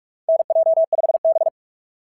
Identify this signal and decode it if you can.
Morse code, heard as a keyed tone, about 44 wpm, 660 Hz. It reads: NJ5B
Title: SOTA QSO Practice 45wpm